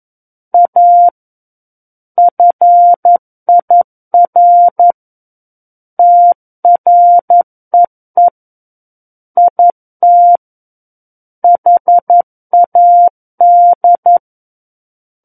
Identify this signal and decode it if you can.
Morse code, heard as a keyed tone, about 11 wpm, 700 Hz. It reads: A FIR TREE IT HAD
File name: 11 WPM